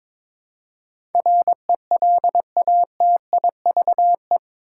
Morse code, heard as a keyed tone, about 22 wpm, 700 Hz. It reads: RELATIVE
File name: practice / 22wpm / Eight Letter Words - 22wpm